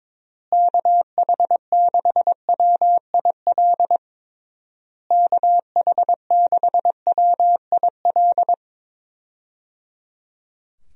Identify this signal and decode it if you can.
Morse code, heard as a keyed tone, about 22 wpm, 700 Hz. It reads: KH6WIL KH6WIL
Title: USA Call Signs - 6 Characters - Repeated 3x - 22wpm